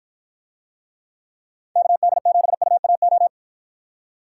Morse code, heard as a keyed tone, about 53 wpm, 700 Hz. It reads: KD7UAO